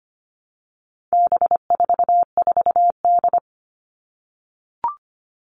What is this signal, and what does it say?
Morse code, heard as a keyed tone, about 25 wpm, 700 Hz. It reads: B44D
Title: All Cut Numbers - 4-Digit 25wpm